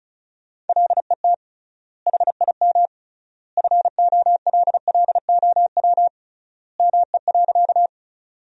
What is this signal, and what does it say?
Morse code, heard as a keyed tone, about 36 wpm, 700 Hz. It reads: LET HIM FOLLOW ME.
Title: 3x Word-Spacing - Sentences from Top 300 Words - 35wpm